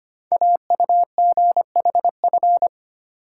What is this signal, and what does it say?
Morse code, heard as a keyed tone, about 25 wpm, 700 Hz. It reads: AUGHF
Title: Code Groups - 5 Letters - 25wpm